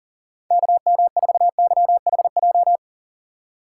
Morse code, heard as a keyed tone, about 40 wpm, 700 Hz. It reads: KM4YHJ
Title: USA Call Signs - 6 Characters - 2x Character Spacing - 40wpm